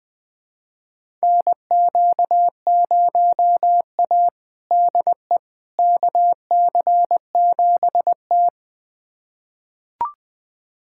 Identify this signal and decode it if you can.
Morse code, heard as a keyed tone, about 20 wpm, 700 Hz. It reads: NQ0A DE KC7T
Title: USA Callsign DE Callsign - 4 Characters - 20wpm